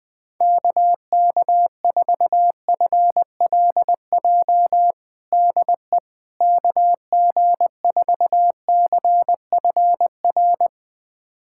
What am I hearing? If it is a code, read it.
Morse code, heard as a keyed tone, about 21 wpm, 700 Hz. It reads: KK4FLJ DE KG4CFR
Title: USA Callsign DE Callsign - 6 Characters - 20wpm